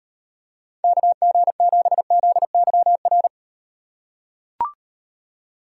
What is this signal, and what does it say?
Morse code, heard as a keyed tone, about 38 wpm, 700 Hz. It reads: KG7ZYR